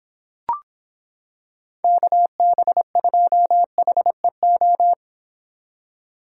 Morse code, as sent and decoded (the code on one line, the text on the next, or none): -.- -... ..--- .... ---
KB2HEO